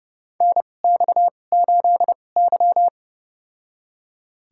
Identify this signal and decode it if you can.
Morse code, heard as a keyed tone, about 30 wpm, spaced as for 22 wpm, 700 Hz. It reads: NX8Y
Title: USA Call Signs - 4 Characters - 2x Character Spacing - 30wpm